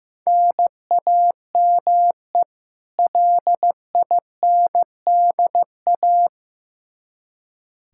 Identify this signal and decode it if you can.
Morse code, heard as a keyed tone, about 15 wpm, 700 Hz. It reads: NAME LINDA
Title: Practice QSO - Exchange 1 - Beginning - 15wpm